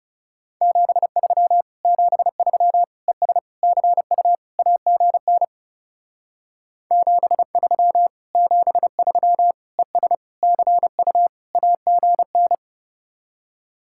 Morse code, heard as a keyed tone, about 37 wpm, 700 Hz. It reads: 73 73 ES CU AGN 73 73 ES CU AGN
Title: Speed-Racing Common QSO Phrases - 30wpm